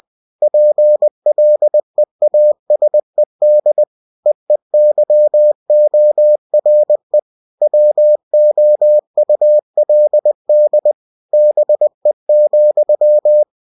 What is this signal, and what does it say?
Morse code, heard as a keyed tone, about 20 wpm, 600 Hz. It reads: PLEASED EEYORE WOULD BE,